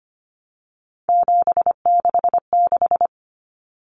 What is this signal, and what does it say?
Morse code, heard as a keyed tone, about 25 wpm, 700 Hz. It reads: 766
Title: Common Cut Numbers - 3-Digit 25wpm